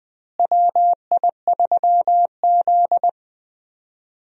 Morse code, heard as a keyed tone, about 20 wpm, 700 Hz. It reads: WI3Z
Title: USA Call Signs - 4 Characters - Repeated 3x - 20wpm